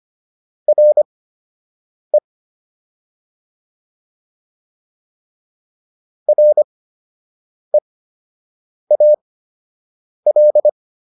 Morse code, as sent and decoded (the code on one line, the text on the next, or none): .-. / .-. .- .-..
RE REAL